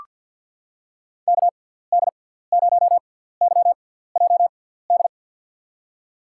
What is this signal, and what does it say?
Morse code, heard as a keyed tone, about 48 wpm, spaced as for 17 wpm, 700 Hz. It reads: KD0YJD
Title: USA Call Signs - 6x Character Spacing - 50wpm